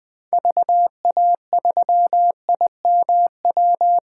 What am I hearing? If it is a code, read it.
Morse code, heard as a keyed tone, about 21 wpm, 700 Hz. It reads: VA3IMW
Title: 2023 POTA Hunter Call Signs - 20wpm